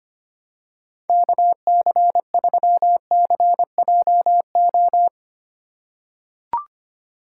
Morse code, heard as a keyed tone, about 25 wpm, 700 Hz. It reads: KC3CJO
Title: USA Call Signs - 6 Characters - 3x Character Spacing - 25wpm